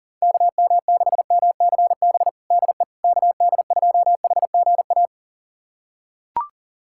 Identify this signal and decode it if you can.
Morse code, heard as a keyed tone, about 40 wpm, 700 Hz. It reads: KM6MCB DE KD2HGA
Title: USA Callsign DE Callsign 40wpm